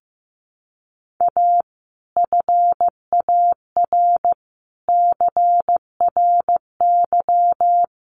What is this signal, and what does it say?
Morse code, heard as a keyed tone, about 15 wpm, 700 Hz. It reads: A FAR CRY
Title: Sets of 5 Words Encore 15wpm